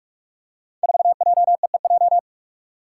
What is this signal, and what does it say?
Morse code, heard as a keyed tone, about 44 wpm, 700 Hz. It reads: 4JEEJ
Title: Five Challenging Letter-Number Combinations 45wpm